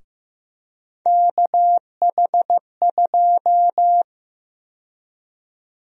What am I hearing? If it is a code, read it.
Morse code, heard as a keyed tone, about 16 wpm, 700 Hz. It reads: KH2
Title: USA Call Sign Prefixes - 15wpm